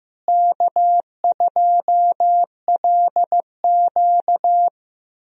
Morse code, heard as a keyed tone, about 15 wpm, 700 Hz. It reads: K2LQ
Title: USA Call Signs - 4 Characters - Repeated 3x - 15wpm